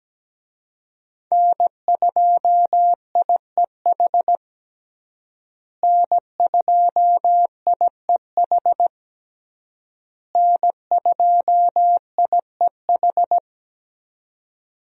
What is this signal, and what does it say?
Morse code, heard as a keyed tone, about 17 wpm, 700 Hz. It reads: N2IEH N2IEH N2IEH
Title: USA Call Signs - 5 Characters - Repeated 3x - 17wpm